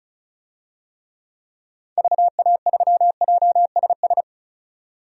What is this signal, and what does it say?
Morse code, heard as a keyed tone, about 36 wpm, 700 Hz. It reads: VA3JSS
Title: DXCC-100 Call Signs - 6 Characters - 35wpm